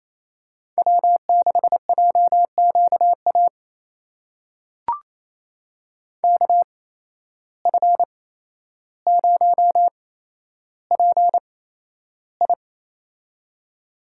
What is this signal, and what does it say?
Morse code, heard as a keyed tone, about 28 wpm, spaced as for 12 wpm, 700 Hz. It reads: W6JQA KF0PI
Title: USA Call Signs - 8x Character Spacing - 28wpm